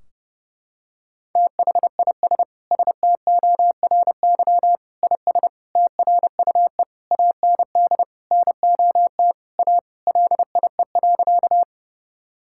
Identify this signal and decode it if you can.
Morse code, heard as a keyed tone, about 30 wpm, 700 Hz. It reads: THIS STORY IS TRUE AND NOT A LIE.
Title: Sentences from Top 1600 Words 30wpm